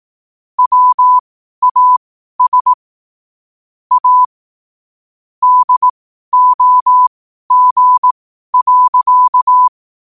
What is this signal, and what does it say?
Morse code, heard as a keyed tone, about 18 wpm, spaced as for 12 wpm, 1000 Hz. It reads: WAS A DOG.
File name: qotdfiles / 12wpm